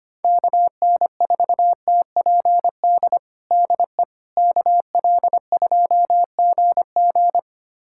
Morse code, heard as a keyed tone, about 25 wpm, 700 Hz. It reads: KN4TPD DE KL2GG
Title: USA Callsign DE Callsign 25wpm